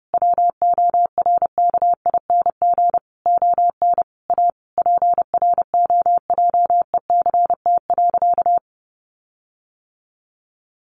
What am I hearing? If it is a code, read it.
Morse code, heard as a keyed tone, about 30 wpm, 700 Hz. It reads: WORKING ON A PROJECT.
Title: Sentences from Top 1400 Words 30wpm